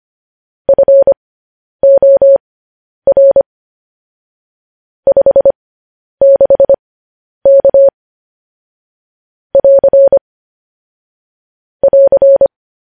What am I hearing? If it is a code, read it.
Morse code, heard as a keyed tone, about 25 wpm, spaced as for 10 wpm, 550 Hz. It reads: FOR 56K <AR> <AR>
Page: news-headlines-in-morse-code-at-10-wpm